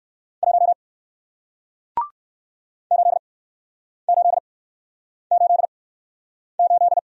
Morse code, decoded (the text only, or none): 1 8888